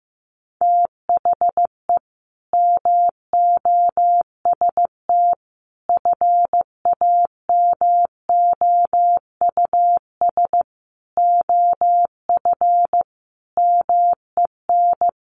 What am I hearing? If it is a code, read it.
Morse code, heard as a keyed tone, about 15 wpm, 700 Hz. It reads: THE MOST FAMOUS OF MEN
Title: Sentences from Top 1500 Words 15wpm